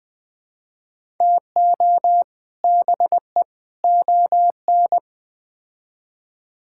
Morse code, heard as a keyed tone, about 20 wpm, 700 Hz. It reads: TO BE ON
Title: Build A Phrase - 4-Word Phrases - 20wpm